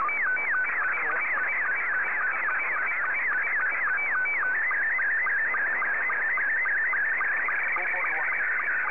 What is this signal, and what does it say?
SSTV (slow-scan television), Scottie 2, with speech.